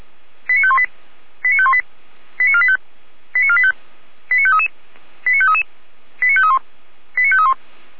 Five-tone selective call (ZVEI), heard as quick runs of five tones, tones of about 70 ms. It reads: 89518 89518 89485 89485 89533 89533 89531 89531